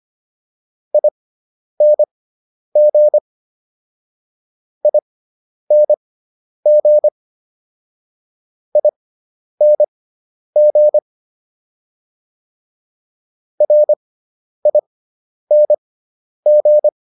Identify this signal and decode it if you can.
Morse code, heard as a keyed tone, about 25 wpm, spaced as for 10 wpm, 600 Hz. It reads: ING ING ING RING